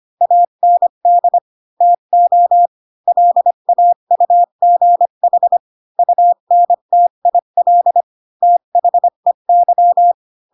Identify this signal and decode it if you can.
Morse code, heard as a keyed tone, about 25 wpm, 700 Hz. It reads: AND TO LAUGH UNTIL THEY
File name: Pinocchio / Pinocchio ch3 20 25 ik1hge